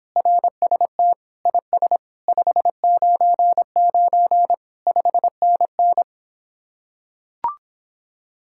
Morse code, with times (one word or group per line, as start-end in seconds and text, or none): 0.2-1.1 RST
1.5-2.0 IS
2.3-4.5 599
4.9-6.0 5NN